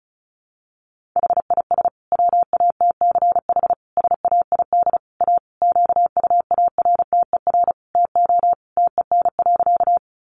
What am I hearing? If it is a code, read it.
Morse code, heard as a keyed tone, about 34 wpm, 700 Hz. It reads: HIS WATCH SAID A QUARTER TO TEN.
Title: Sentences from Top 1300 Words 35wpm